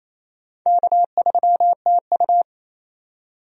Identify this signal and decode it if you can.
Morse code, heard as a keyed tone, about 28 wpm, 700 Hz. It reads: K3TU
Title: USA Call Signs - 4 Characters - 28wpm